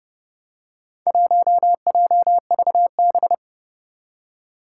Morse code, heard as a keyed tone, about 32 wpm, 700 Hz. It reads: 1JVB
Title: Four Challenging Letter-Number Combinations 30wpm